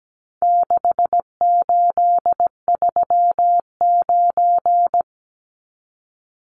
Morse code, heard as a keyed tone, about 17 wpm, 700 Hz. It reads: 6839
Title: Numbers - 4-Digit 17wpm